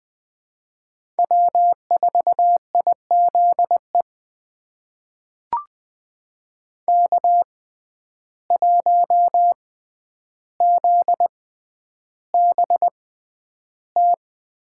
Morse code, heard as a keyed tone, about 20 wpm, spaced as for 10 wpm, 700 Hz. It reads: W4IZE K1ZBT
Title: USA Call Signs - 5 Characters - 6x Character Spacing - 20wpm